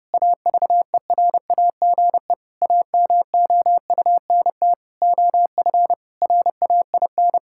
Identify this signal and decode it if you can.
Morse code, heard as a keyed tone, about 30 wpm, 700 Hz. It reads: AVERAGE AMOUNT OF RAIN
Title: Sentences from Top 1000 Words - Review Part 6 - 30wpm